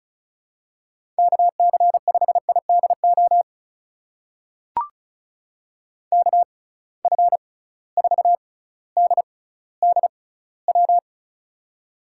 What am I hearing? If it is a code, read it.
Morse code, heard as a keyed tone, about 34 wpm, spaced as for 18 wpm, 690 Hz. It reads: KC5IDO KF4DDW
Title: USA Call Signs - 6 Characters - 6x Character Spacing - 35wpm